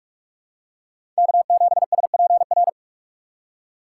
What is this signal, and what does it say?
Morse code, heard as a keyed tone, about 44 wpm, 700 Hz. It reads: K7SPR
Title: USA Call Signs - 5 Characters - 6x Character Spacing - 45wpm